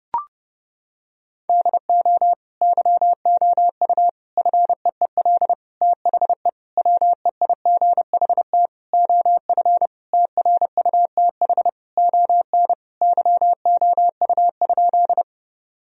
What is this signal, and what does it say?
Morse code, heard as a keyed tone, about 32 wpm, 700 Hz. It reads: DO YOU FEEL THE WEIGHT OF TRUTH ON YOU?
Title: Sentences from Top 1000 Words - Review Part 3 - 30wpm